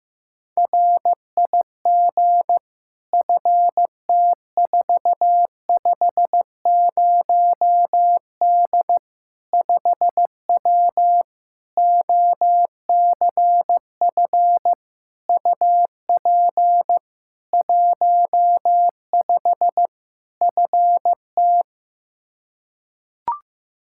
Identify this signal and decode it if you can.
Morse code, heard as a keyed tone, about 15 wpm, 700 Hz. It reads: RIG FT450D 5W OCF UP 15 FT
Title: K1USN SST QSO Practice 15wpm